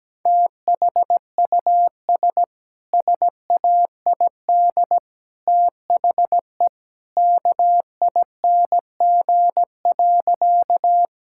Morse code, heard as a keyed tone, about 17 wpm, 700 Hz. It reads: THUS SAID THE KING.
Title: Sentences from Top 1000 Words - Review Part 5 - 17wpm